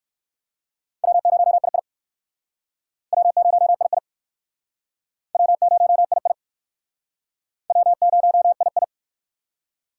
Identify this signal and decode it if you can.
Morse code, heard as a keyed tone, about 60 wpm, 700 Hz. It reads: W0II W0II W0II W0II